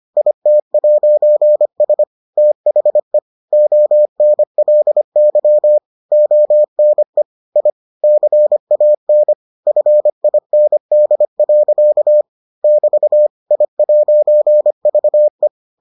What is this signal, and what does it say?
Morse code, heard as a keyed tone, about 25 wpm, 600 Hz. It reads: IT'S THE ONLY ONE I CAN FIND. <BT> I'VE